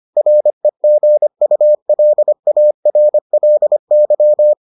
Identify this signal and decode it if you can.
Morse code, heard as a keyed tone, about 25 wpm, 600 Hz. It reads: REGULARLY